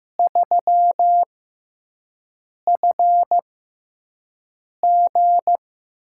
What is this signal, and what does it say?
Morse code, heard as a keyed tone, about 15 wpm, spaced as for 5 wpm, 700 Hz. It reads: 3FG
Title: USA Call Signs - 6x Character Spacing - 15wpm